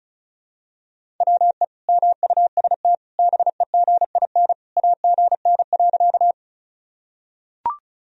Morse code, heard as a keyed tone, about 34 wpm, 700 Hz. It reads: WE MUST BEGIN AGN.
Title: Sentences from Top 300 Words and Abbreviations 35wpm